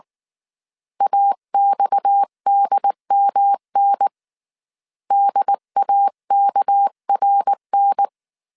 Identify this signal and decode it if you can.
Morse code, heard as a keyed tone, about 19 wpm, 800 Hz. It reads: AXDMN DAKRN
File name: CW / DET / 1 9 1500C S C 18wpm 2539